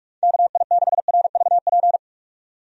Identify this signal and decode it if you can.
Morse code, heard as a keyed tone, about 46 wpm, 700 Hz. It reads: KI6RVP
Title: USA Call Signs - 6 Characters - 45wpm